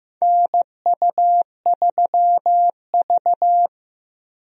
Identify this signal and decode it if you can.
Morse code, heard as a keyed tone, about 15 wpm, 700 Hz. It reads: NU3V
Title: USA Call Signs - 4 Characters - Repeated 3x - 15wpm